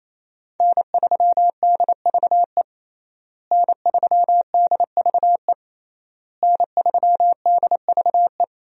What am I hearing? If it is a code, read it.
Morse code, heard as a keyed tone, about 28 wpm, 700 Hz. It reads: N3DVE N3DVE N3DVE
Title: USA Call Signs - 5 Characters - Repeated 3x - 28wpm